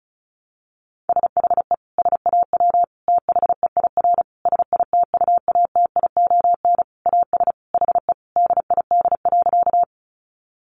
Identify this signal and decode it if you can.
Morse code, heard as a keyed tone, about 34 wpm, 700 Hz. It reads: SHE SAW THEIR SITUATION AS HE DID.
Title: Sentences from Top 1200 Words 35wpm